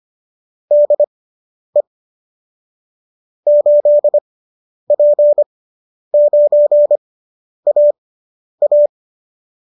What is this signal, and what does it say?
Morse code, heard as a keyed tone, about 25 wpm, spaced as for 10 wpm, 590 Hz. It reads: DE 8P9AA